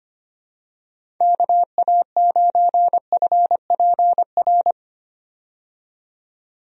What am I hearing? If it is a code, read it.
Morse code, heard as a keyed tone, about 25 wpm, 700 Hz. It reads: KA9FPR
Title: USA Call Signs - 6 Characters - 25wpm